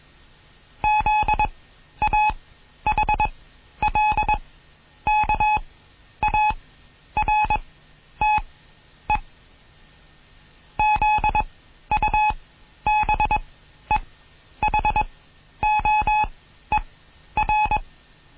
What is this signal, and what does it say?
Morse code, heard as a keyed tone, about 21 wpm, spaced as for 11 wpm, 850 Hz. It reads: ZAHLKARTE ZUBEHOER